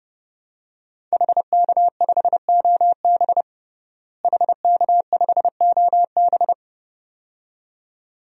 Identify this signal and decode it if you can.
Morse code, heard as a keyed tone, about 32 wpm, 700 Hz. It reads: HK5OB HK5OB
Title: DXCC-100 Call Signs - 5 Characters - Repeated 2x - 30wpm